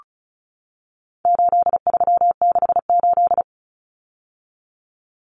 Morse code, heard as a keyed tone, about 34 wpm, 700 Hz. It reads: 8368
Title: Numbers - 4-Digit 35wpm